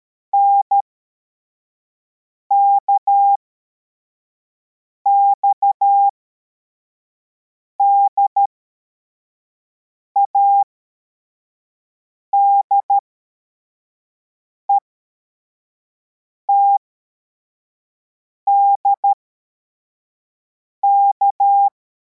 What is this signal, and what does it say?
Morse code, heard as a keyed tone, about 13 wpm, spaced as for 4 wpm, 800 Hz. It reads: NKXDA DETDK
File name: formation / cw / Text23